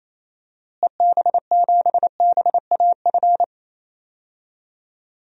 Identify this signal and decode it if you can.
Morse code, heard as a keyed tone, about 28 wpm, 700 Hz. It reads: EB7BAF
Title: DXCC-100 Call Signs - 6 Characters - Repeated 3x - 28wpm